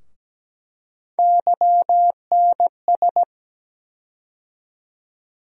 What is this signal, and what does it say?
Morse code, heard as a keyed tone, about 17 wpm, 700 Hz. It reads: YNS
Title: USA Call Sign Postfixes - 3 Letters 17wpm